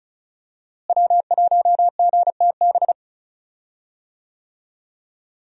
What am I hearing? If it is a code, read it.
Morse code, heard as a keyed tone, about 34 wpm, 700 Hz. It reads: W1GTB
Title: USA Call Signs - 5 Characters - Repeated 2x - 35wpm